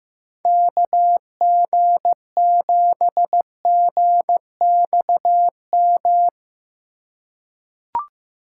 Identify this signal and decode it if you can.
Morse code, heard as a keyed tone, about 16 wpm, 700 Hz. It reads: KG7GXM
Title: USA Call Signs - 6 Characters - 3x Character Spacing - 15wpm